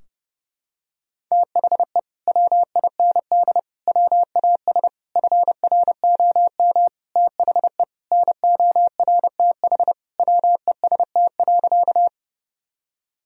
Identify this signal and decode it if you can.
Morse code, heard as a keyed tone, about 30 wpm, 700 Hz. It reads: THE WIND WAS FROM THE NORTH WEST.
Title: Sentences from Top 1000 Words - Review Part 2 - 30wpm